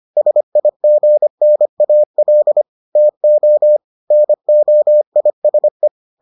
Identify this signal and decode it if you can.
Morse code, heard as a keyed tone, about 25 wpm, 600 Hz. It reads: SIGNAL TO NOISE